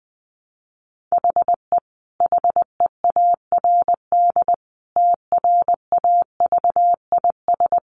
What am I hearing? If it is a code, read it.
Morse code, heard as a keyed tone, about 20 wpm, 700 Hz. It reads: HE HEARD TRAVIS
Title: A Sound Of Thunder 20wpm